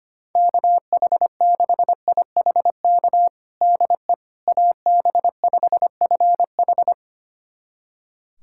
Morse code, heard as a keyed tone, about 25 wpm, 700 Hz. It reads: KH6IHK DE AB5FH
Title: USA Callsign DE Callsign 25wpm